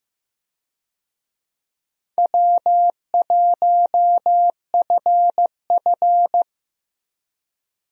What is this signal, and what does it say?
Morse code, heard as a keyed tone, about 15 wpm, 700 Hz. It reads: W1FF
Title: USA Call Signs - 4 Characters - 15wpm